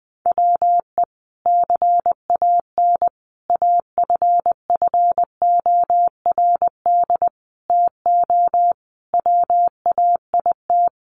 Morse code, heard as a keyed tone, about 20 wpm, 700 Hz. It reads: WE CAN AFFORD TO WAIT
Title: Sentences from Top 1100 Words 20wpm